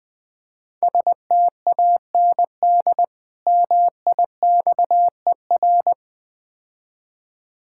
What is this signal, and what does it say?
Morse code, heard as a keyed tone, about 20 wpm, 700 Hz. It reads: STAND MIXER
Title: Easy Words - Speed-Racing with Hints - 20wpm